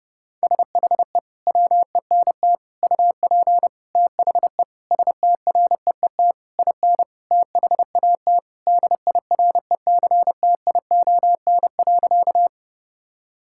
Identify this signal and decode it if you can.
Morse code, heard as a keyed tone, about 32 wpm, 700 Hz. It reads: SHE WENT UP THE STREET IN THAT DIRECTION.
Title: Sentences from Top 1000 Words - Review Part 5 - 30wpm